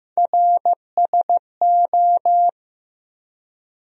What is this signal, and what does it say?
Morse code, heard as a keyed tone, about 15 wpm, 700 Hz. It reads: RSO
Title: USA Call Sign Postfixes - 3 Letters 15wpm